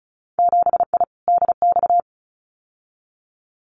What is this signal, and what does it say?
Morse code, heard as a keyed tone, about 34 wpm, 700 Hz. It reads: 7I DX
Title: Winter Field Day Exchanges Encore 35wpm